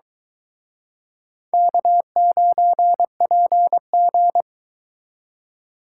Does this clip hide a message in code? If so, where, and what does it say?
Yes: Morse code from 1.5 to 4.4 s, heard as a keyed tone, about 23 wpm, 700 Hz. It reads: K9PG